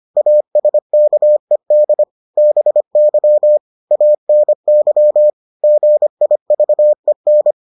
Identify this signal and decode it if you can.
Morse code, heard as a keyed tone, about 25 wpm, 600 Hz. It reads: ASKED BY ANY GIVEN